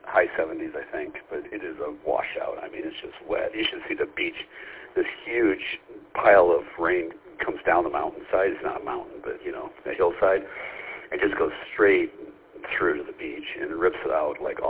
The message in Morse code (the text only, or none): none